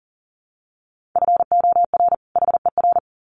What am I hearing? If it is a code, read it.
Morse code, heard as a keyed tone, about 40 wpm, 700 Hz. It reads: FOR HER
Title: Sets of 2 Words 40wpm